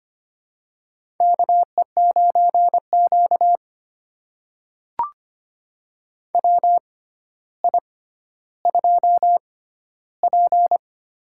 Morse code, heard as a keyed tone, about 25 wpm, spaced as for 13 wpm, 700 Hz. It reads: KE9Q WI2P
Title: USA Call Signs - 4 Characters - 6x Character Spacing - 25wpm